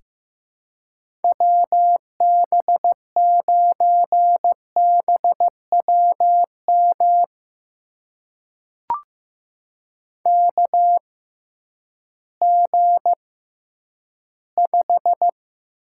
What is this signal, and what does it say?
Morse code, heard as a keyed tone, about 15 wpm, spaced as for 10 wpm, 700 Hz. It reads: WB9BWM KG5
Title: USA Call Signs - 6 Characters - 6x Character Spacing - 15wpm